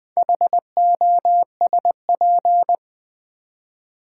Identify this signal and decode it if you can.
Morse code, heard as a keyed tone, about 20 wpm, 700 Hz. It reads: HOSP
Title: Build A Word - 2 Letters At A Time - 8-Letter Words - 20wpm